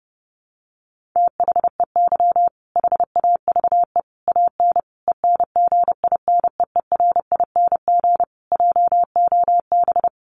Morse code, heard as a keyed tone, about 30 wpm, 700 Hz. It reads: THEY HAVE AN ENGINEERING JOB
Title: Sentences from Top 1500 Words 30wpm